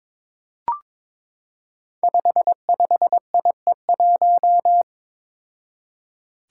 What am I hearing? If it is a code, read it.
Morse code, heard as a keyed tone, about 21 wpm, 700 Hz. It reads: 55IE1
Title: Five Challenging Letter-Number Combinations 22wpm